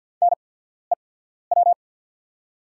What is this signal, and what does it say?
Morse code, heard as a keyed tone, about 48 wpm, spaced as for 14 wpm, 690 Hz. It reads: NEW
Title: 8x Character Spacing - Top 500 Words - 50wpm